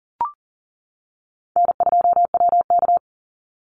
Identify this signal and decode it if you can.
Morse code, heard as a keyed tone, about 38 wpm, 700 Hz. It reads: N2WK
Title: Call Signs - 40wpm